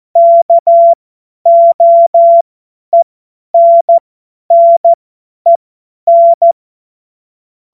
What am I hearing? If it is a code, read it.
Morse code, heard as a keyed tone, about 14 wpm, spaced as for 10 wpm, 680 Hz. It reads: KOENNEN
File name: Morsekurs / Witze / Witz 24. Januar 2023 50 F70